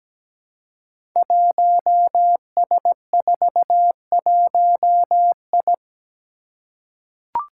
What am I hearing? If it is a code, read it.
Morse code, heard as a keyed tone, about 17 wpm, 700 Hz. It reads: 1S41I
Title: Five Challenging Letter-Number Combinations 17wpm